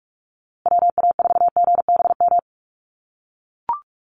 Morse code, heard as a keyed tone, about 44 wpm, 700 Hz. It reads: WA4GBM
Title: USA Call Signs Repeated 3x 45wpm